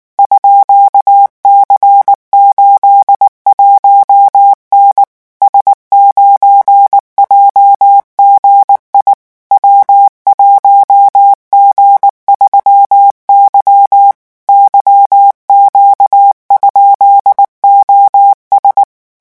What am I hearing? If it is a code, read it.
Morse code, heard as a keyed tone, about 19 wpm, 800 Hz. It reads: _C81N S9JGI W1G3Y YQ?OS
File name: morse mp3 / revision12-vitesse 18 mots